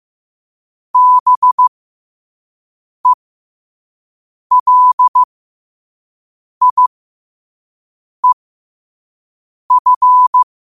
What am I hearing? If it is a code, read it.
Morse code, heard as a keyed tone, about 15 wpm, spaced as for 5 wpm, 1000 Hz. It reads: BELIEF
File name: 5wpm